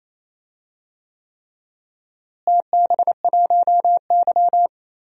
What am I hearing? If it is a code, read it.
Morse code, heard as a keyed tone, about 28 wpm, 700 Hz. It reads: TB1Y
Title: DXCC-100 Call Signs - 4 Characters - Repeated 3x - 28wpm